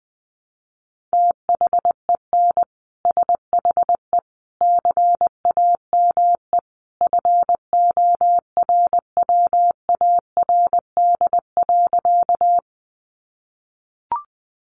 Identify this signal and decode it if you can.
Morse code, heard as a keyed tone, about 20 wpm, 700 Hz. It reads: THEN SHE CAME FORWARD.
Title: Sentences from Top 1300 Words 20wpm